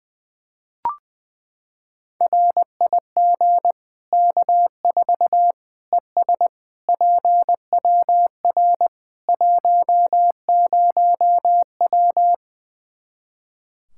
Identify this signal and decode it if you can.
Morse code, heard as a keyed tone, about 21 wpm, 700 Hz. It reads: RIG K4 ES PWR 10W
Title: Practice QSO - Exchange 2 - Info - 20wpm